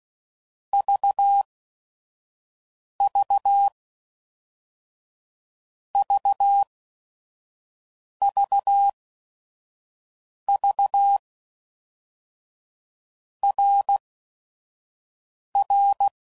Morse code, heard as a keyed tone, about 16 wpm, 800 Hz. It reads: V V V V V R R
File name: Basic Letter Groups By7Groups / RKPX / RKPX@5 f15